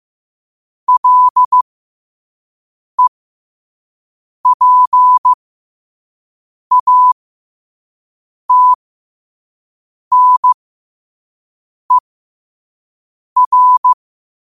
Morse code, heard as a keyed tone, about 15 wpm, spaced as for 5 wpm, 1000 Hz. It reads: LEPATNER